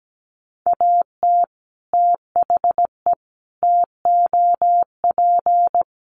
Morse code, heard as a keyed tone, about 17 wpm, 700 Hz. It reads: AT THE TOP
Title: Sets of 4 Words 17wpm